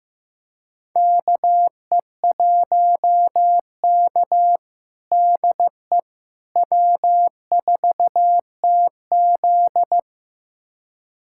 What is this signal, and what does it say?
Morse code, heard as a keyed tone, about 15 wpm, 700 Hz. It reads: KE1K DE W4TZ
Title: USA Callsign DE Callsign - 4 Characters - 15wpm